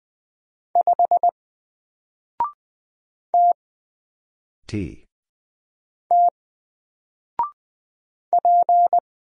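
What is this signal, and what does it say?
Morse code, heard as a keyed tone, about 20 wpm, 700 Hz. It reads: 5 T T P
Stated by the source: Single letter and number Morse drill at 20 wpm, from N4LCD.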